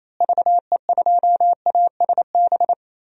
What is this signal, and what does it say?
Morse code, heard as a keyed tone, about 28 wpm, 700 Hz. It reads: VE2ASB
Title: DXCC-100 Call Signs - 6 Characters - Repeated 3x - 28wpm